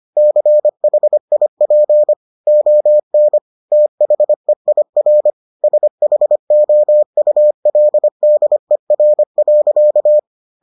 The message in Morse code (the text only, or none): CHIP ON THEIR SHOULDER.